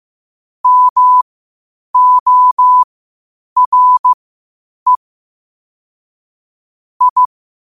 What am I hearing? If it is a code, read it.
Morse code, heard as a keyed tone, about 15 wpm, spaced as for 8 wpm, 1000 Hz. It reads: MORE I
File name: qotdfiles / 8wpm